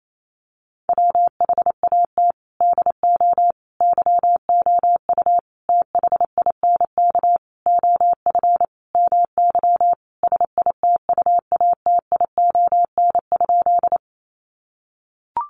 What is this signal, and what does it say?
Morse code, heard as a keyed tone, about 28 wpm, 700 Hz. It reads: WHAT DO YOU THINK OF MY SITUATION?
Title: Sentences from Top 1200 Words 28wpm